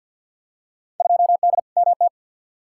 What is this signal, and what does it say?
Morse code, heard as a keyed tone, about 48 wpm, 700 Hz. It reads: 2D MT
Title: Field Day Exchanges Encore - 50wpm